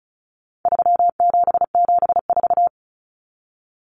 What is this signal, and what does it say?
Morse code, heard as a keyed tone, about 34 wpm, 700 Hz. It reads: 3774
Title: Common Cut Numbers - 4-Digit 35wpm